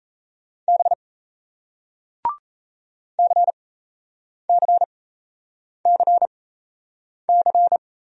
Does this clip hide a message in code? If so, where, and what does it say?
Yes: Morse code from 0.7 to 7.8 s, heard as a keyed tone, about 37 wpm, spaced as for 8 wpm, 700 Hz. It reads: B CCCC